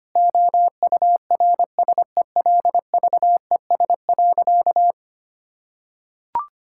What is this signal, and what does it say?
Morse code, heard as a keyed tone, about 25 wpm, 700 Hz. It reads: OURSELVES.
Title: Sentences from Top 1600 Words 25wpm